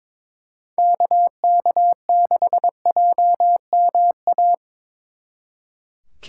Morse code, heard as a keyed tone, about 24 wpm, 700 Hz. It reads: KK6JMA
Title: Speed Racing - USA 2x3 Call Signs 22wpm